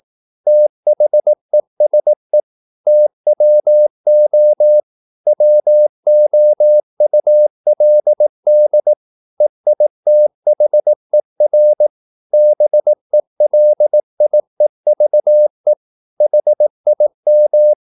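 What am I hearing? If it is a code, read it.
Morse code, heard as a keyed tone, about 18 wpm, 600 Hz. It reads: THESE TWO WOULD EITHER BELIEVE HIM
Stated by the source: Morse practice at 18 wpm, English text, from CW med Gnister.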